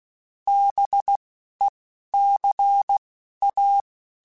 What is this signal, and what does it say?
Morse code, heard as a keyed tone, about 16 wpm, spaced as for 11 wpm, 800 Hz. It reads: BECA